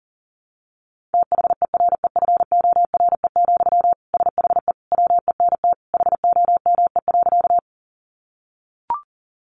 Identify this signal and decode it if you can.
Morse code, heard as a keyed tone, about 40 wpm, 700 Hz. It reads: THEREFORE, SHE WENT HOME.
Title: Sentences from Top 800 Words 40wpm